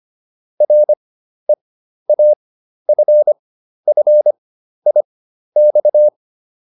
Morse code, heard as a keyed tone, about 25 wpm, spaced as for 12 wpm, 600 Hz. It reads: REAFFIX